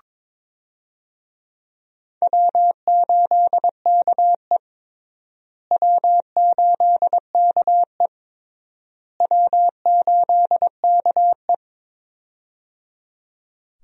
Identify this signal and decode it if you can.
Morse code, heard as a keyed tone, about 21 wpm, 700 Hz. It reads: W8KE W8KE W8KE
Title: USA Call Signs - 4 Characters - Repeated 3x - 22wpm